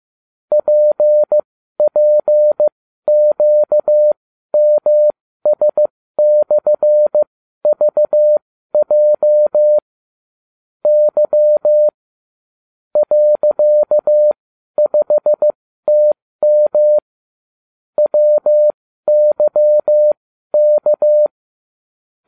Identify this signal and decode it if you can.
Morse code, heard as a keyed tone, about 15 wpm, spaced as for 12 wpm, 600 Hz. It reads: PPQMS/VJ Y .5TM WYK